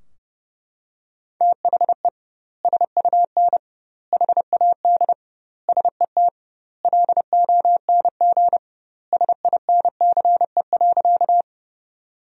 Morse code, heard as a keyed tone, about 32 wpm, 700 Hz. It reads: THE SUN HAD SET LONG SINCE.
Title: 2x Word-Spacing - Sentences from Top 300 Words - 30wpm